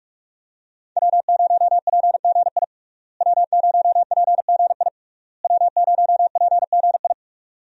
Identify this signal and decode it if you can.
Morse code, heard as a keyed tone, about 43 wpm, 700 Hz. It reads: W0PGI W0PGI W0PGI
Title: USA Call Signs - 5 Characters - Repeated 3x - 45wpm